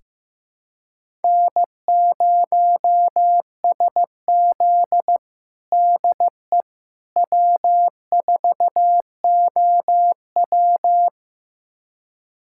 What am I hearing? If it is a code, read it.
Morse code, heard as a keyed tone, about 15 wpm, 700 Hz. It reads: N0SZ DE W4OW
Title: USA Callsign DE Callsign - 4 Characters - 15wpm